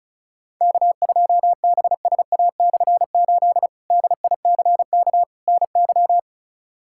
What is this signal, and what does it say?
Morse code, heard as a keyed tone, about 34 wpm, 700 Hz. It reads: K2BSA/8 DICK NY
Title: K1USN SST QSO Practice - 35wpm